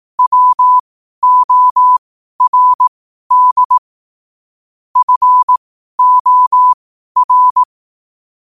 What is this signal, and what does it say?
Morse code, heard as a keyed tone, about 18 wpm, spaced as for 12 wpm, 1000 Hz. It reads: WORD FOR
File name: qotdfiles / 12wpm